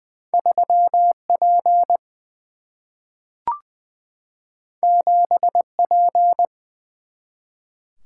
Morse code, heard as a keyed tone, about 20 wpm, 700 Hz. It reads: 3 P 7 P
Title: Single-Letter Single-Number 20wpm